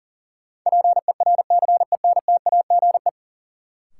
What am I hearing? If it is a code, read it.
Morse code, heard as a keyed tone, about 40 wpm, 700 Hz. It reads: PERCENTAGE
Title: Sentences from Top 1300 Words - New Word Drill - 40wpm